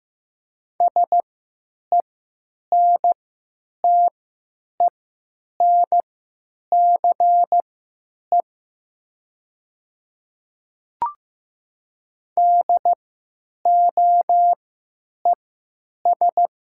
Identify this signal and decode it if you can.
Morse code, heard as a keyed tone, about 15 wpm, spaced as for 9 wpm, 700 Hz. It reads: SENTENCE DOES
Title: 3x Character Spacing - Top 200 Words - 15wpm